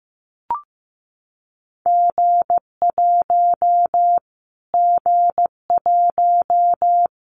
Morse code, heard as a keyed tone, about 15 wpm, 700 Hz. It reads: G1 G1
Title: Callsign Prefixes for Top 25 DXCC - Repeated 3x - 15wpm